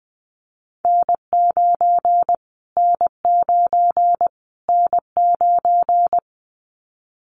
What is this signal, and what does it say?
Morse code, heard as a keyed tone, about 20 wpm, 700 Hz. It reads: N9 N9 N9
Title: Callsign Prefixes for Top 25 DXCC - Repeated 3x - 20wpm